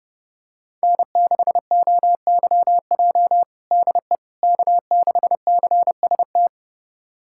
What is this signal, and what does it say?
Morse code, heard as a keyed tone, about 32 wpm, 700 Hz. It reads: N6OYJ DE K6CST
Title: USA Callsign DE Callsign - 5 Characters - 30wpm